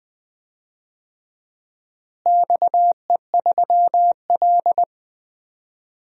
Morse code, heard as a keyed tone, about 20 wpm, 700 Hz. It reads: XE3L
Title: DXCC-100 Call Signs - 4 Characters - Repeated 3x - 20wpm